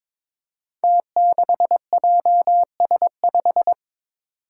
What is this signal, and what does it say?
Morse code, heard as a keyed tone, about 22 wpm, 700 Hz. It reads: T6JS5
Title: Five Challenging Letter-Number Combinations 22wpm